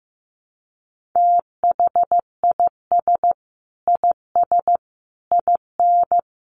Morse code, heard as a keyed tone, about 15 wpm, 700 Hz. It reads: THIS IS IN
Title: Sentences from Top 1500 Words 15wpm